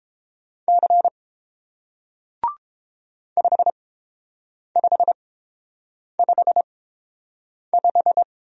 Morse code, heard as a keyed tone, about 30 wpm, spaced as for 7 wpm, 700 Hz. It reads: C 5555